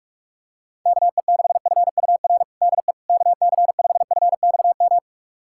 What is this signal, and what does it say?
Morse code, heard as a keyed tone, about 41 wpm, 700 Hz. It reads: KE6FUR DE KC5FXM
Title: USA Callsign DE Callsign 45wpm